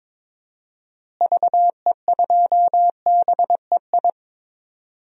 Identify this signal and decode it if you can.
Morse code, heard as a keyed tone, about 22 wpm, 700 Hz. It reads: VE2BEI
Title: DXCC-100 Call Signs - 6 Characters - Repeated 3x - 22wpm